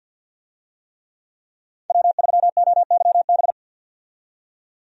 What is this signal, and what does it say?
Morse code, heard as a keyed tone, about 49 wpm, 700 Hz. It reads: W3OYB
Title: USA Call Signs - 5 Characters - 50wpm